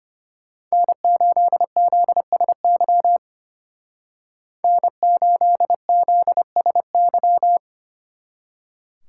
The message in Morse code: N8ZSY N8ZSY

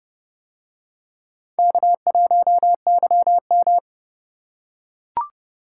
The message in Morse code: K1YM

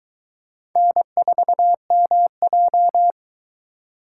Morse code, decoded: N4MJ

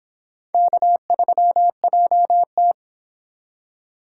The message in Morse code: K3JT

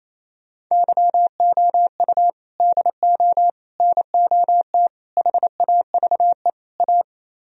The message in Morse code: YOU DO NOT HAVE A

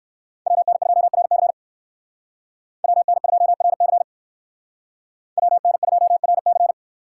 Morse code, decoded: WN2RC WN2RC WN2RC

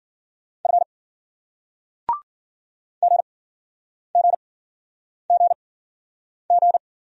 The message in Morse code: F G G G G

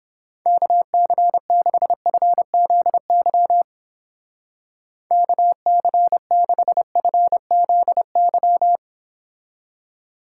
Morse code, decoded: KC6FZY KC6FZY

